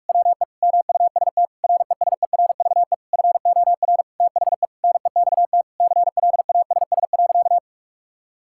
Morse code, heard as a keyed tone, about 46 wpm, 700 Hz. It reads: WE MUST RESERVE FOR THE NEXT CLASS.
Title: Sentences from Top 1000 Words - Review Part 6 - 45wpm